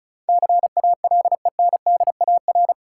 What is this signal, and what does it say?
Morse code, heard as a keyed tone, about 36 wpm, 700 Hz. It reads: CALENDAR
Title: Sentences from Top 1600 Words - New Word Drill - 35wpm